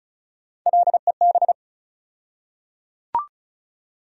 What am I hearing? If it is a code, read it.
Morse code, heard as a keyed tone, about 34 wpm, 700 Hz. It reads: LEB
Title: USA Call Sign Postfixes - 3 Letters 35wpm